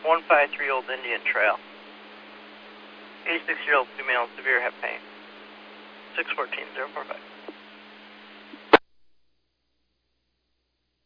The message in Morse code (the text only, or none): none